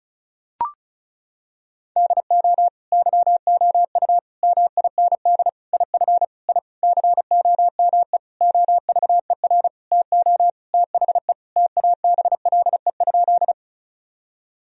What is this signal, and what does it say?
Morse code, heard as a keyed tone, about 36 wpm, 700 Hz. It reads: DO YOU MIND IF I COME OVER TO THE TABLE?
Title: Sentences from Top 1000 Words - Review Part 2 - 35wpm